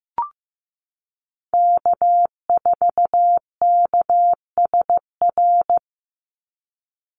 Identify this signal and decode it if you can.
Morse code, heard as a keyed tone, about 15 wpm, 700 Hz. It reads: K4KSR
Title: Call Signs Repeated 2x - 15wpm